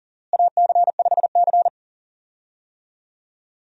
Morse code, heard as a keyed tone, about 40 wpm, 700 Hz. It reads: AC5C
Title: USA Call Signs - 4 Characters - Repeated 2x - 40wpm